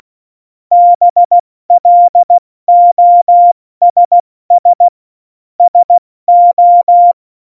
Morse code, heard as a keyed tone, about 16 wpm, 700 Hz. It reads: BLOSS SO